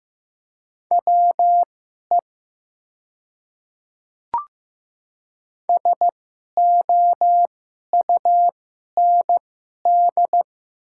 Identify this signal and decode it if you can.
Morse code, heard as a keyed tone, about 15 wpm, spaced as for 11 wpm, 700 Hz. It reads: WE SOUND
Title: 2x Character Spacing - Top 500 Words - 15wpm